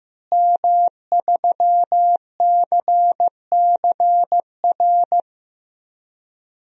Morse code, heard as a keyed tone, about 15 wpm, 700 Hz. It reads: M3CCR
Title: DXCC-100 Call Signs - 5 Characters - Repeated 2x - 15wpm